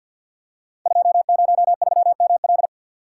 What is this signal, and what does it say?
Morse code, heard as a keyed tone, about 48 wpm, 700 Hz. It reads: 203ML
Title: Code Groups - 5 Letter:Numbers - 50wpm